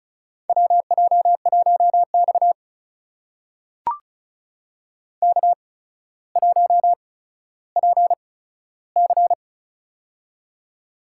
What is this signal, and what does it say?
Morse code, heard as a keyed tone, about 36 wpm, spaced as for 15 wpm, 700 Hz. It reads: WJ1X K1PC
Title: USA Call Signs - 4 Characters - 8x Character Spacing - 35wpm